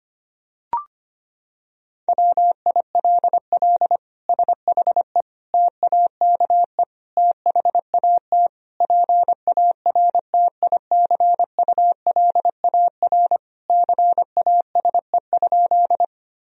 Morse code, heard as a keyed tone, about 25 wpm, 700 Hz. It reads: WILL SHE TAKE THAT PARTICULAR CASE?
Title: Sentences from Top 1000 Words - Review Part 4 - 25wpm